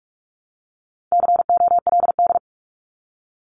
Morse code, heard as a keyed tone, about 45 wpm, 700 Hz. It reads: COLD
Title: Top 500 Words Repeated 45wpm